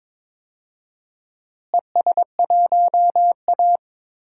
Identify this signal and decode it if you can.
Morse code, heard as a keyed tone, about 22 wpm, 700 Hz. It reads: ES1A
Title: DXCC-100 Call Signs - 4 Characters - Repeated 3x - 22wpm